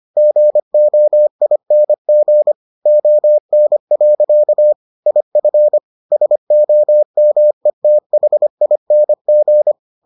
Morse code, heard as a keyed tone, about 25 wpm, 600 Hz. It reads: GOING ON. IF SOMETHING